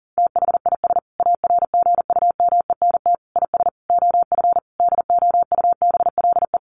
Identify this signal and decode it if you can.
Morse code, heard as a keyed tone, about 40 wpm, 700 Hz. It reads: THIS ARGUMENT IS OF DOUBLE
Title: Sentences from Top 1400 Words 40wpm